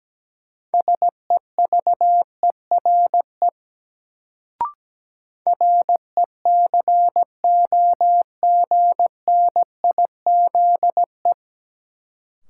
Morse code, heard as a keyed tone, about 17 wpm, 700 Hz. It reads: SEVERE RECOGNIZE
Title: Sentences from Top 1400 Words - New Word Drill - 17wpm